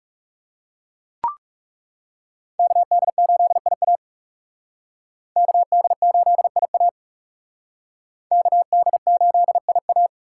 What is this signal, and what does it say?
Morse code, heard as a keyed tone, about 40 wpm, 700 Hz. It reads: KD8IA KD8IA KD8IA